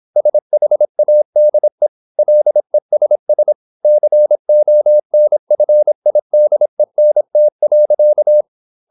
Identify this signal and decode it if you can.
Morse code, heard as a keyed tone, about 26 wpm, 600 Hz. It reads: SHADE LESS CONFIDENT.
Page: Never 26wpm | CW med Gnister